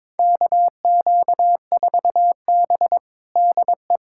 Morse code, heard as a keyed tone, about 22 wpm, 700 Hz. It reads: KQ4B DE
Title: USA Callsign DE Callsign - 4 Characters - 22wpm